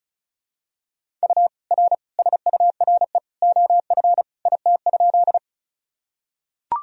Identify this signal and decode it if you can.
Morse code, heard as a keyed tone, about 36 wpm, 700 Hz. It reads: U R SURE OF IT?
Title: Sentences from Top 300 Words and Abbreviations 35wpm